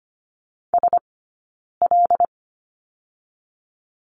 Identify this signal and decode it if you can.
Morse code, heard as a keyed tone, about 25 wpm, 700 Hz. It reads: S L
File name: Lesson 17d - Letter u - Two Characters 25wpm